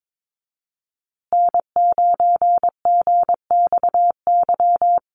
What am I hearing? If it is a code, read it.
Morse code, heard as a keyed tone, about 22 wpm, 700 Hz. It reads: N9GXY
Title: USA Call Signs Repeated 3x 22wpm